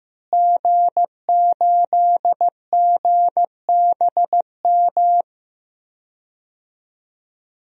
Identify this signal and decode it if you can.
Morse code, heard as a keyed tone, about 16 wpm, 700 Hz. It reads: G8GBM
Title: DXCC-100 Call Signs - 5 Characters - Repeated 2x - 15wpm